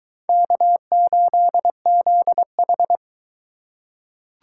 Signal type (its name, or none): Morse code